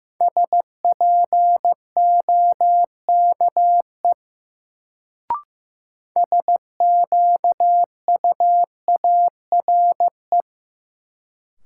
Morse code, heard as a keyed tone, about 15 wpm, 700 Hz. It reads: SPOKE SQUARE
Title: Sentences from Top 1200 Words - New Word Drill Alt - 15wpm